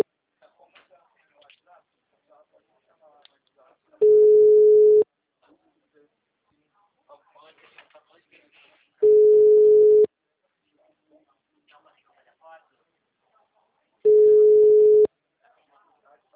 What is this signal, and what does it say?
Ringback tone (Europe, 425 Hz), with speech.